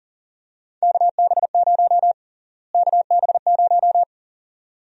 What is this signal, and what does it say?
Morse code, heard as a keyed tone, about 40 wpm, 700 Hz. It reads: KB0 KB0
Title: USA Call Sign Prefixes - Repeated 2x - 40wpm